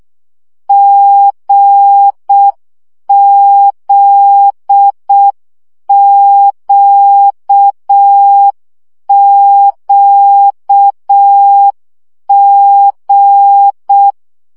Morse code, heard as a keyed tone, about 6 wpm, 800 Hz. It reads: GZQQG
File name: CW gyakorlas g z q 5 baud